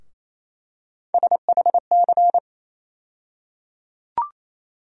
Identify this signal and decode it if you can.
Morse code, heard as a keyed tone, about 28 wpm, 700 Hz. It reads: SHC
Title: USA Call Sign Postfixes - 3 Letters 28wpm